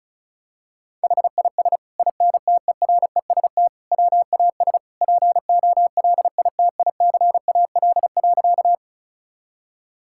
Morse code, heard as a keyed tone, about 34 wpm, 690 Hz. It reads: HIS INTEREST WAS POLITICAL.
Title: Sentences from Top 1000 Words - Review Part 5 - 35wpm